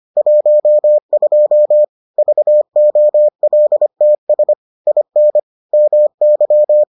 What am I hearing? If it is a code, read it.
Morse code, heard as a keyed tone, about 25 wpm, 600 Hz. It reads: 12 VOLTS IN MY